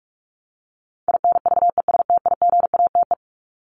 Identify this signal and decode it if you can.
Morse code, heard as a keyed tone, about 44 wpm, 700 Hz. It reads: INVESTIGATE
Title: Top 1751-2000 Words 45wpm